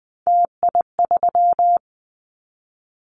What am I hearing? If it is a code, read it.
Morse code, heard as a keyed tone, about 20 wpm, 700 Hz. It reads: TI3
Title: Callsign Prefixes for Top 100 DXCC - 20wpm